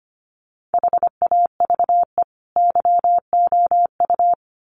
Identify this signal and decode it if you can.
Morse code, heard as a keyed tone, about 25 wpm, 700 Hz. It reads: HAVE YOU
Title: Sentences from Top 700 Words 25wpm